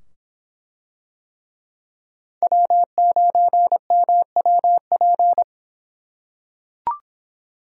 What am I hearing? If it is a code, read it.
Morse code, heard as a keyed tone, about 23 wpm, 700 Hz. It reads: W9MWP